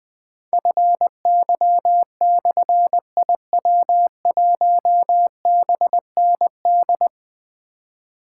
Morse code, heard as a keyed tone, about 20 wpm, 700 Hz. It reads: FY/IW1BND
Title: Difficult Call Signs 20wpm